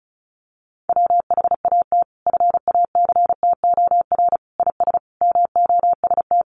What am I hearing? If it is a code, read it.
Morse code, heard as a keyed tone, about 34 wpm, 700 Hz. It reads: WHAT FACTOR IS MOST
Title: Sentences from Top 1300 Words 35wpm